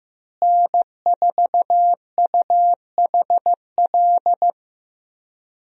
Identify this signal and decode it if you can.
Morse code, heard as a keyed tone, about 15 wpm, 700 Hz. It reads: N4UHL
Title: USA Call Signs - 5 Characters - Repeated 3x - 15wpm